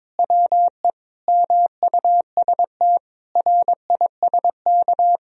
Morse code, heard as a keyed tone, about 21 wpm, 700 Hz. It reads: WE MUST RISK
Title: Sentences from Top 1000 Words - Review Part 4 - 22wpm